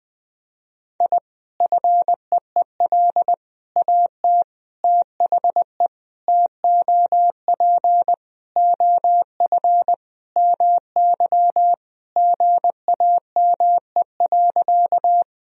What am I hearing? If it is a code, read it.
Morse code, heard as a keyed tone, about 21 wpm, 700 Hz. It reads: I FEEL AT THE TOP OF MY GAME.
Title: Sentences from Top 1000 Words - Review Part 2 - 20wpm